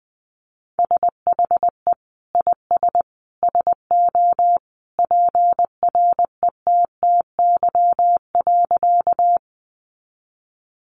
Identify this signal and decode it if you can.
Morse code, heard as a keyed tone, about 20 wpm, 700 Hz. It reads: SHE IS SO PRETTY.
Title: Sentences from Top 1000 Words 20wpm